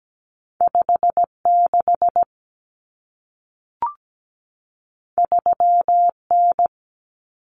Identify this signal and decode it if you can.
Morse code, heard as a keyed tone, about 17 wpm, 700 Hz. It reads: 56 3N
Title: Common Cut Numbers - 2-Digit 17wpm